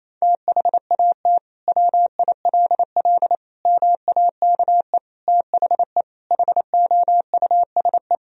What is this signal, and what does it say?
Morse code, heard as a keyed tone, about 28 wpm, 700 Hz. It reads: THAT WILL MAKE THE HOUSE
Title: Sentences from Top 1000 Words - Review Part 3 - 28wpm